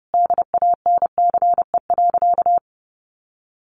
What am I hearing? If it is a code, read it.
Morse code, heard as a keyed tone, about 30 wpm, 700 Hz. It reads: DANCE.
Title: Easy Phrases 30wpm